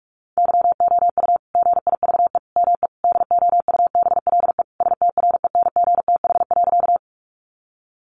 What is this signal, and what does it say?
Morse code, heard as a keyed tone, about 44 wpm, 700 Hz. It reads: YOU GIVE ME DOUBLE STRENGTH.
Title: Sentences from Top 1400 Words 45wpm